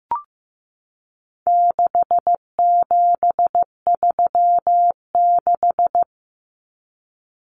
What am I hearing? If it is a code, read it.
Morse code, heard as a keyed tone, about 14 wpm, 700 Hz. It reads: 6736
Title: Common Cut Numbers - 4-Digit 15wpm